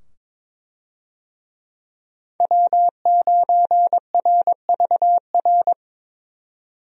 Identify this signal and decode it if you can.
Morse code, heard as a keyed tone, about 22 wpm, 700 Hz. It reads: W9RVR
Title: USA Call Signs - 5 Characters - 22wpm